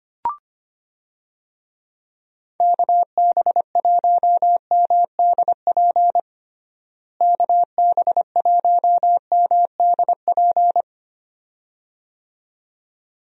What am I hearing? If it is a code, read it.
Morse code, heard as a keyed tone, about 25 wpm, 700 Hz. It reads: KB1MDP KB1MDP
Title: USA Call Signs - 6 Characters - Repeated 2x - 25wpm